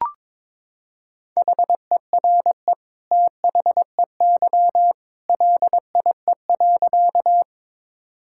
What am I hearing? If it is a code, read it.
Morse code, heard as a keyed tone, about 22 wpm, 700 Hz. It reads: HERE THEY LIE.
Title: Sentences from Top 1600 Words 22wpm